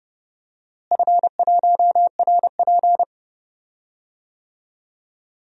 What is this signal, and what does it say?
Morse code, heard as a keyed tone, about 31 wpm, 700 Hz. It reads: F1RP
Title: DXCC-100 Call Signs - 4 Characters - Repeated 3x - 30wpm